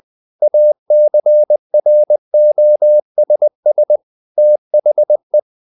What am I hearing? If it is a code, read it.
Morse code, heard as a keyed tone, about 20 wpm, 600 Hz. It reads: ACROSS THE
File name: the old man and the sea part 2 20wpm